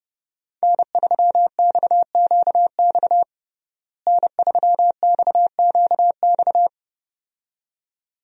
Morse code, heard as a keyed tone, about 30 wpm, 700 Hz. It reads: N3XQX N3XQX
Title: USA Call Signs - 5 Characters - Repeated 2x - 30wpm